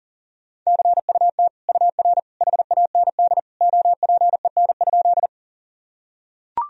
Morse code, heard as a keyed tone, about 40 wpm, 700 Hz. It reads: CUT UR HAND OPEN?
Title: Sentences from Top 300 Words and Abbreviations 40wpm